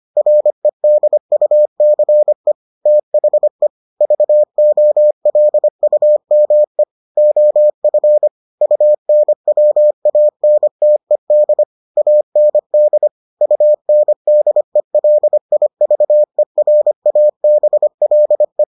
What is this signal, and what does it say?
Morse code, heard as a keyed tone, about 25 wpm, 600 Hz. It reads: REDUCE THE VOLUME OF UNWANTED AND UNDELIVERABLE